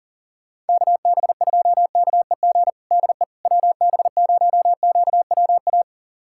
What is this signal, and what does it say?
Morse code, heard as a keyed tone, about 40 wpm, 700 Hz. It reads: KB2KEG DE WB0QWA
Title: USA Callsign DE Callsign 40wpm